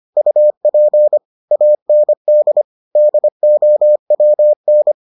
Morse code, heard as a keyed tone, about 25 wpm, 600 Hz. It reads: UP AND DOWN